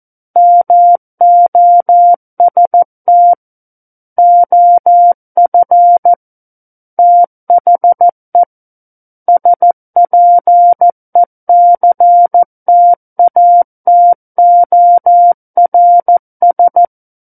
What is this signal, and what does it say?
Morse code, heard as a keyed tone, about 14 wpm, 700 Hz. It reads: MOST OF THE SPECTATORS
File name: Book One / 05-Chapter 5 - 14 WPM